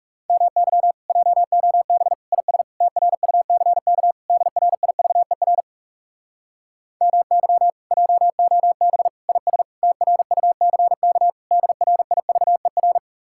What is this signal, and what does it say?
Morse code, heard as a keyed tone, about 40 wpm, 700 Hz. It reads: MY JOB IS TRUCK DRIVER MY JOB IS TRUCK DRIVER